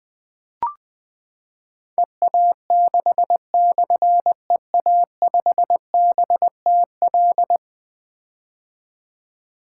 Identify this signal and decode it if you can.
Morse code, heard as a keyed tone, about 20 wpm, 700 Hz. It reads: EA6/EA5BTL
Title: Difficult Call Signs 20wpm